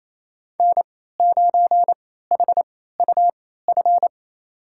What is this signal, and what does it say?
Morse code, heard as a keyed tone, about 28 wpm, spaced as for 16 wpm, 700 Hz. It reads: N9HUF
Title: USA Call Signs - 3x Character Spacing - 28wpm